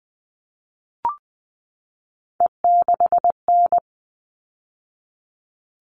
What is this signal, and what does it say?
Morse code, heard as a keyed tone, about 20 wpm, 700 Hz. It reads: E6N
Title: All Cut Numbers - 3-Digit 20wpm